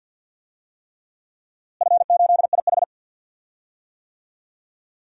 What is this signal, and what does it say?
Morse code, heard as a keyed tone, about 47 wpm, 700 Hz. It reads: F8IH
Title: DXCC-100 Call Signs - 4 Characters - Repeated 3x - 50wpm